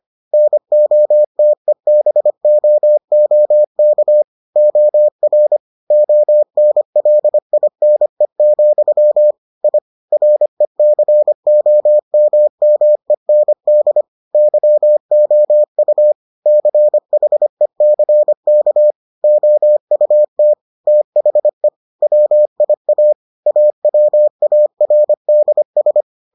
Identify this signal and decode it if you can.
Morse code, heard as a keyed tone, about 25 wpm, 600 Hz. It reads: NOTEBOOK OR ONLINE, I RECOMMEND YOU CHECK OUT THE WIA AWARDS